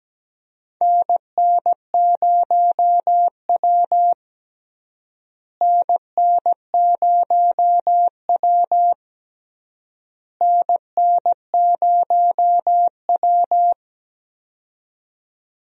Morse code, heard as a keyed tone, about 17 wpm, 700 Hz. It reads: NN0W NN0W NN0W
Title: USA Call Signs - 4 Characters - Repeated 3x - 17wpm